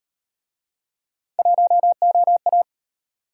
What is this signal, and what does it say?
Morse code, heard as a keyed tone, about 33 wpm, 700 Hz. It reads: 1OA